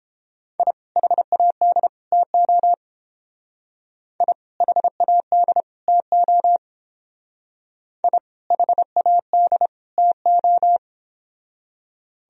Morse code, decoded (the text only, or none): I HAD TO I HAD TO I HAD TO